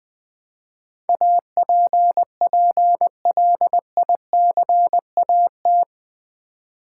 Morse code, heard as a keyed tone, about 20 wpm, 700 Hz. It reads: APPLICAT
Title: Build A Word - 4 Letters At A Time - 11-Letter Words - 20wpm